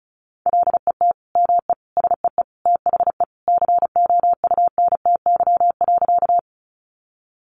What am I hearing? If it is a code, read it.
Morse code, heard as a keyed tone, about 34 wpm, 700 Hz. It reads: LET ME SEE THE COUNTY.
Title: Sentences from Top 1200 Words 35wpm